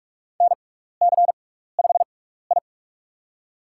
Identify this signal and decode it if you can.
Morse code, heard as a keyed tone, about 46 wpm, spaced as for 16 wpm, 700 Hz. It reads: NC5I
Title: USA Call Signs - 4 Characters - 6x Character Spacing - 45wpm